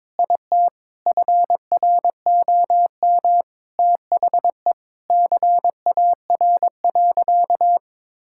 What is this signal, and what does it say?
Morse code, heard as a keyed tone, about 22 wpm, 700 Hz. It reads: IT FROM THE CAR.
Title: Sentences from Top 1000 Words - Review Part 5 - 22wpm